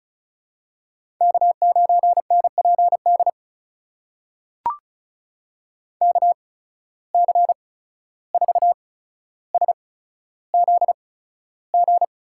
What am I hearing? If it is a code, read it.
Morse code, heard as a keyed tone, about 36 wpm, spaced as for 14 wpm, 700 Hz. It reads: K9NPD KC4SZG
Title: USA Call Signs - 8x Character Spacing - 35wpm